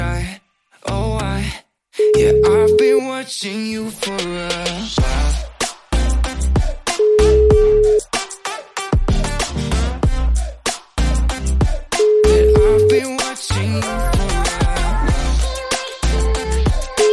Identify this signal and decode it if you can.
Ringback tone (Europe, 425 Hz), with speech.